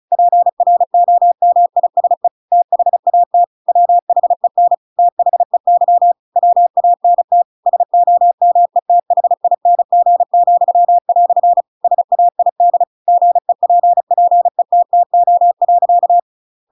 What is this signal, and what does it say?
Morse code, heard as a keyed tone, about 34 wpm, 700 Hz. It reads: PROMISE THAT WHEN THEY WANT SOMETHING," SAID GEPPETTO.
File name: Pinocchio / Pinocchio ch8 35 35 ik1hge